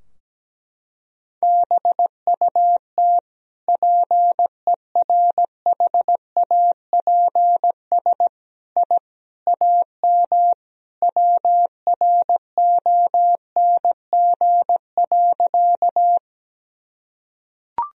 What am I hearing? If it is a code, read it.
Morse code, heard as a keyed tone, about 17 wpm, 700 Hz. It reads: BUT PERHAPS I AM WRONG.
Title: Sentences from Top 1000 Words - Review Part 2 - 17wpm